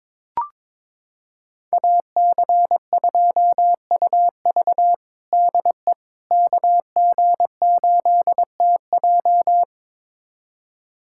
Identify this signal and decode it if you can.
Morse code, heard as a keyed tone, about 21 wpm, 700 Hz. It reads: AC2UV DE KG8TJ
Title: USA Callsign DE Callsign - 5 Characters - 22wpm